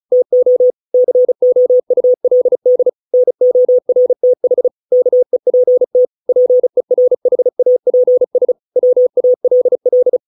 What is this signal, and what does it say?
Morse code, heard as a keyed tone, about 34 wpm, 500 Hz. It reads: TO COULD NORTH KEPT PERHAPS WALL